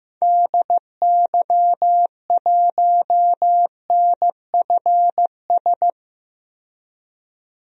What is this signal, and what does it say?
Morse code, heard as a keyed tone, about 15 wpm, 690 Hz. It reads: DY1NFS
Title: DXCC-100 Call Signs - 6 Characters - 15wpm